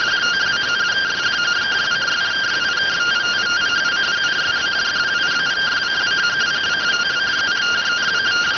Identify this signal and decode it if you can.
Two-tone teleprinter signal, about 170 Hz shift.